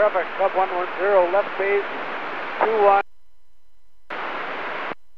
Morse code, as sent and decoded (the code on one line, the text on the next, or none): none